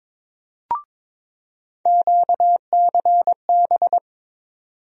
Morse code, heard as a keyed tone, about 22 wpm, 700 Hz. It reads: QCB
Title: USA Call Sign Postfixes - 3 Letters 22wpm